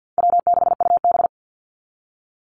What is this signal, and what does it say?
Morse code, heard as a keyed tone, about 50 wpm, 700 Hz. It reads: W6UB
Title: NA SOTA Activator Callsigns 50wpm